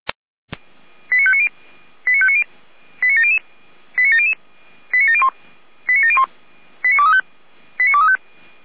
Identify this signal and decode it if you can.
Five-tone selective call (ZVEI), heard as quick runs of five tones, tones of about 70 ms. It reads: 89590 89590 89700 89700 89701 89701 89235 89235